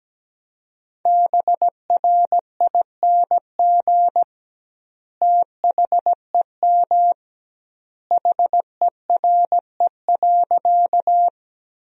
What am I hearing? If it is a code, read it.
Morse code, heard as a keyed tone, about 17 wpm, 700 Hz. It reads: BRING THEM HERE.
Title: 2x Word-Spacing - Sentences from Top 500 Words - 17wpm